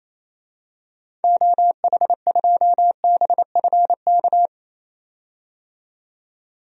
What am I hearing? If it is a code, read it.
Morse code, heard as a keyed tone, about 28 wpm, 700 Hz. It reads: OH2BFK
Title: DXCC-100 Call Signs - 6 Characters - 28wpm